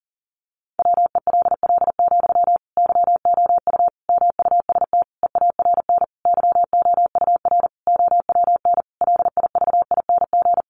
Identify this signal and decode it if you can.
Morse code, heard as a keyed tone, about 40 wpm, 700 Hz. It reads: WELL, YOU MUST EARN YOUR OWN LIVING
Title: Sentences from Top 1500 Words 40wpm